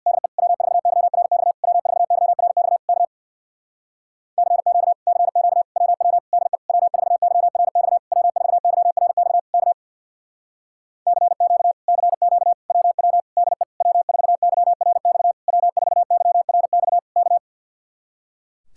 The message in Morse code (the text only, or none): DE W4YRX W4YRX K CQ CQ WW DE W4YRX W4YRX K CQ CQ WW DE W4YRX W4YRX K